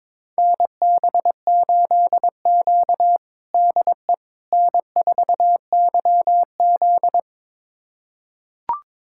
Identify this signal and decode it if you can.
Morse code, heard as a keyed tone, about 21 wpm, 700 Hz. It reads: NB8Q DE N4YZ
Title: USA Callsign DE Callsign - 4 Characters - 22wpm